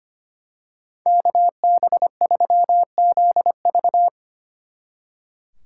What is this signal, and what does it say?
Morse code, heard as a keyed tone, about 28 wpm, 700 Hz. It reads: KB3ZV
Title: Speed Racing - USA 2x2 Call Signs 25wpm